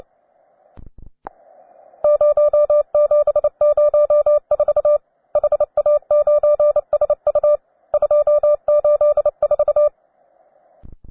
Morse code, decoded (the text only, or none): 0704 HA9SU 284